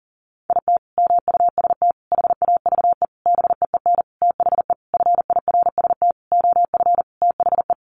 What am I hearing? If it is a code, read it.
Morse code, heard as a keyed tone, about 40 wpm, 700 Hz. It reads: IT MUST HAVE BEEN THE FIRST OF THE
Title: Sentences from Top 800 Words 40wpm